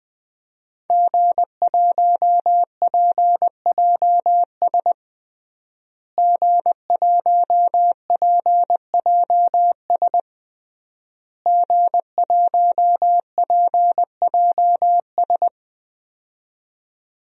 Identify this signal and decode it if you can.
Morse code, heard as a keyed tone, about 21 wpm, 700 Hz. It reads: G1PJS G1PJS G1PJS
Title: DXCC-100 Call Signs - 5 Characters - Repeated 3x - 20wpm